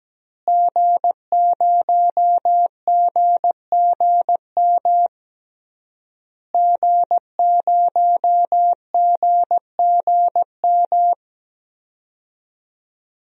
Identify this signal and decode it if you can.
Morse code, heard as a keyed tone, about 17 wpm, 700 Hz. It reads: G0GGM G0GGM
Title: DXCC-100 Call Signs - 5 Characters - Repeated 3x - 17wpm